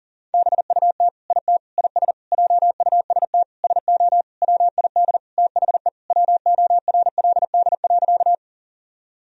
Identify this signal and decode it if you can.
Morse code, heard as a keyed tone, about 40 wpm, 700 Hz. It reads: BUT IT IS JUST SO WID THE WORLD.
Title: Sentences from Top 200 Words and Abbreviations 40wpm